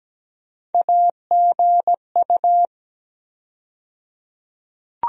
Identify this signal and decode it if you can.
Morse code, heard as a keyed tone, about 17 wpm, 700 Hz. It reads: AGU
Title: USA Call Sign Postfixes - 3 Letters 17wpm